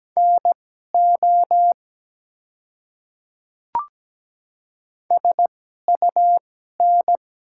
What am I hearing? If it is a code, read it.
Morse code, heard as a keyed tone, about 17 wpm, spaced as for 12 wpm, 690 Hz. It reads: NO SUN
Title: 2x Character Spacing - Top 300 Words - 17wpm